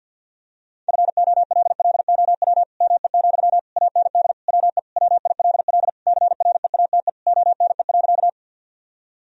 Morse code, heard as a keyed tone, about 50 wpm, 700 Hz. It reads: FOLLOW ME, AND WE WILL CREATE ONE.
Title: Sentences from Top 1000 Words - Review Part 3 - 50wpm